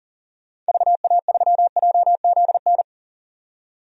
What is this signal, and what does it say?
Morse code, heard as a keyed tone, about 37 wpm, 700 Hz. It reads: VA3JZN
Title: DXCC-100 Call Signs - 6 Characters - 40wpm